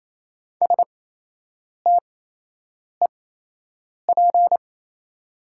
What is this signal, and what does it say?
Morse code, heard as a keyed tone, about 28 wpm, spaced as for 8 wpm, 700 Hz. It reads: STEP
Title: 8x Character Spacing - Top 500 Words - 28wpm